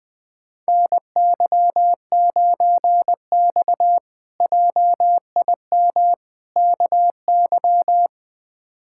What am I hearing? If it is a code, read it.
Morse code, heard as a keyed tone, about 20 wpm, 700 Hz. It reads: NY9X JIM KY
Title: K1USN SST QSO Practice - 20wpm